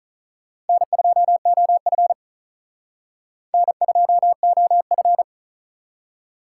Morse code, heard as a keyed tone, about 45 wpm, 700 Hz. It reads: N2OF N2OF